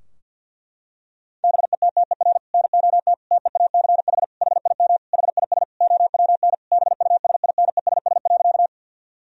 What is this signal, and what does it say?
Morse code, heard as a keyed tone, about 49 wpm, 700 Hz. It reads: BETTER NOT TEACH HIM HIS OWN BUSINESS.
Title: Sentences from Top 1000 Words - Review Part 2 - 50wpm